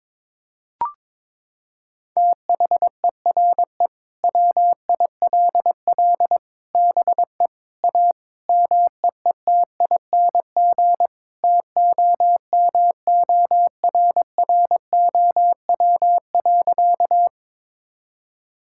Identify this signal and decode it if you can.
Morse code, heard as a keyed tone, about 22 wpm, 700 Hz. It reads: THERE WILL BE A MEETING TOMORROW.
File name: practice / 22wpm / Sentences with 4-Word Phrases - 22wpm